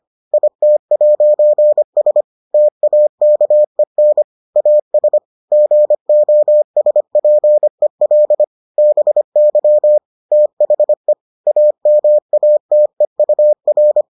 Morse code, heard as a keyed tone, about 25 wpm, 600 Hz. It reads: IT'S TAKEN AS GOSPEL BY THE AMATEUR